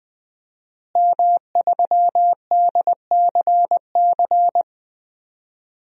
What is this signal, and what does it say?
Morse code, heard as a keyed tone, about 20 wpm, 700 Hz. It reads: M3DCC
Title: DXCC-100 Call Signs - 5 Characters - Repeated 3x - 20wpm